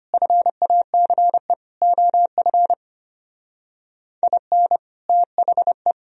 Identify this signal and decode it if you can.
Morse code, heard as a keyed tone, about 30 wpm, 700 Hz. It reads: FACE OF IN THE
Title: Speed-Racing - Sets of 4 Words 25wpm